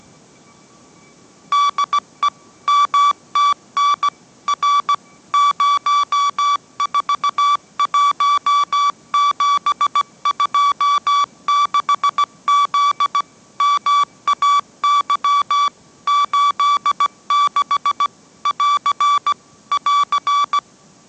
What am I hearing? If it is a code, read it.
Morse code, heard as a keyed tone, about 17 wpm, 1150 Hz. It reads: DE MTN R 041726Z MAY 86 <AR> <AR>